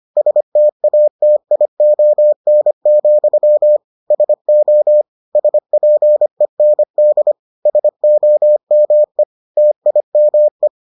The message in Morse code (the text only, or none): STATION, SO SPEND SOME TIME